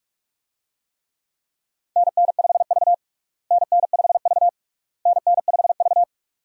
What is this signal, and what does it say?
Morse code, heard as a keyed tone, about 43 wpm, 700 Hz. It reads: NN5V NN5V NN5V
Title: USA Call Signs - 4 Characters - Repeated 3x - 45wpm